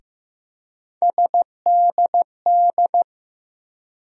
Morse code, heard as a keyed tone, about 15 wpm, 700 Hz. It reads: SDD
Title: USA Call Sign Postfixes - 3 Letters 15wpm